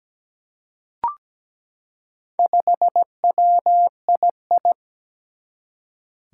Morse code, heard as a keyed tone, about 17 wpm, 700 Hz. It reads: 5WII
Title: Four Challenging Letter-Number Combinations 17wpm